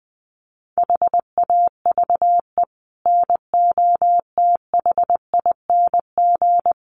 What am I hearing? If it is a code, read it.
Morse code, heard as a keyed tone, about 20 wpm, 700 Hz. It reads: HAVE NOTHING
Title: Sets of 5 Words 20wpm